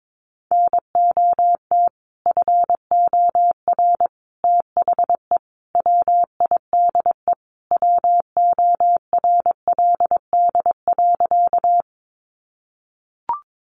Morse code, heard as a keyed tone, about 22 wpm, 700 Hz. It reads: NOT FOR THE WIDE WORLD.
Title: Sentences from Top 900 Words 22wpm